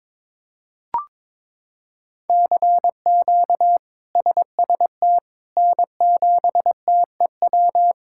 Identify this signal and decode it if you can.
Morse code, heard as a keyed tone, about 22 wpm, 700 Hz. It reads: CQ SST N7TEW
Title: K1USN SST QSO Practice - 22wpm